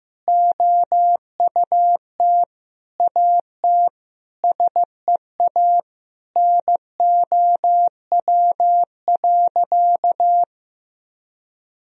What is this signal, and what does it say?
Morse code, heard as a keyed tone, about 16 wpm, 700 Hz. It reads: OUT AT SEA NOW.
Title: Sentences from Top 1000 Words - Review Part 1 - 15wpm